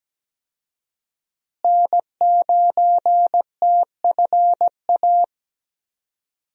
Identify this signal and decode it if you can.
Morse code, heard as a keyed tone, about 17 wpm, 700 Hz. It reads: N9TFA
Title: USA Call Signs - 5 Characters - Repeated 3x - 17wpm